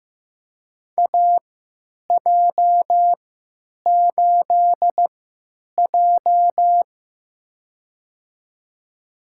Morse code, heard as a keyed tone, about 15 wpm, spaced as for 9 wpm, 700 Hz. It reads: AJ8J
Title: USA Call Signs - 4 Characters - 3x Character Spacing - 15wpm